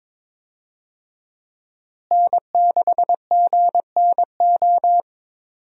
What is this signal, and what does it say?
Morse code, heard as a keyed tone, about 22 wpm, 700 Hz. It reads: N6GNO
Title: USA Call Signs - 5 Characters - Repeated 3x - 22wpm